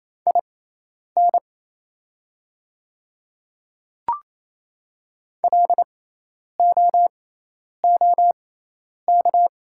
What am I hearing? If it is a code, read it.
Morse code, heard as a keyed tone, about 28 wpm, spaced as for 10 wpm, 690 Hz. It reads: IN LOOK
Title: 6x Character Spacing - Top 100 Words - 28wpm